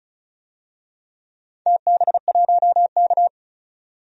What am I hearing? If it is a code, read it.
Morse code, heard as a keyed tone, about 36 wpm, 700 Hz. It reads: TB1K
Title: DXCC-100 Call Signs - 4 Characters - 35wpm